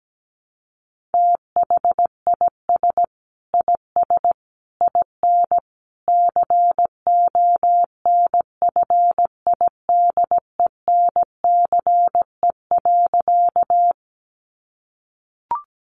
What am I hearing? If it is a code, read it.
Morse code, heard as a keyed tone, about 17 wpm, 700 Hz. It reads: THIS IS IN CONFIDENCE.
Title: Sentences from Top 1500 Words 17wpm